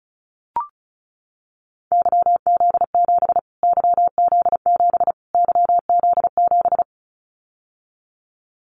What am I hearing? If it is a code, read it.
Morse code, heard as a keyed tone, about 34 wpm, 700 Hz. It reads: YZ7 YZ7 YZ7
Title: Callsign Prefixes for Top 25 DXCC - Repeated 3x - 35wpm